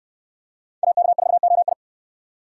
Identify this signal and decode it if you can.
Morse code, heard as a keyed tone, about 65 wpm, 700 Hz. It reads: AK4OI